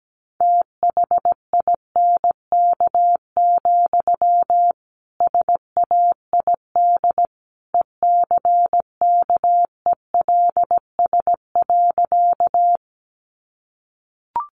Morse code, heard as a keyed tone, about 17 wpm, 700 Hz. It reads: THINK, SAID ECKELS.
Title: A Sound Of Thunder 17wpm